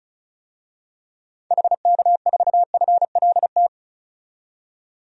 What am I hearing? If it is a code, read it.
Morse code, heard as a keyed tone, about 34 wpm, 700 Hz. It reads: HK4FLT
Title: DXCC-100 Call Signs - 6 Characters - 35wpm